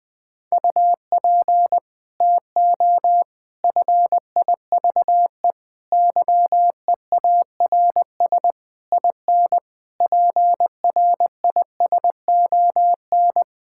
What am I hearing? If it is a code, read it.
Morse code, heard as a keyed tone, about 20 wpm, 700 Hz. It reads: UP TO FIVE YEARS IN PRISON
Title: Sets of 6 Words - 20wpm